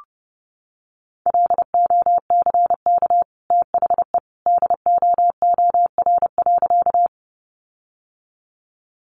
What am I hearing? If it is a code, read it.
Morse code, heard as a keyed tone, about 30 wpm, 700 Hz. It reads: LOCK THE DOOR.
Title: Sentences from Top 1300 Words 30wpm